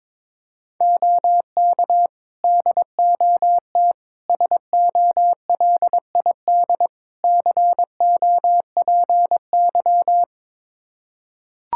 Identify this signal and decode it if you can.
Morse code, heard as a keyed tone, about 22 wpm, 700 Hz. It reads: OK DOT SOLID COPY
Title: Practice QSO - Exchange 3 - More Info - 22wpm